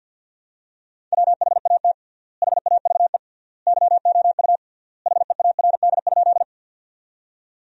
Morse code, heard as a keyed tone, about 50 wpm, 700 Hz. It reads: WHAT HAVE YOU HEARD?
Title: 3x Word-Spacing - Sentences from Top 400 Words - 50wpm